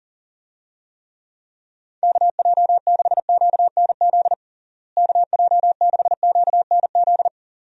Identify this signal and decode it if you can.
Morse code, heard as a keyed tone, about 40 wpm, 700 Hz. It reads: KJ6QNZ KJ6QNZ
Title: USA Call Signs - 6 Characters - Repeated 2x - 40wpm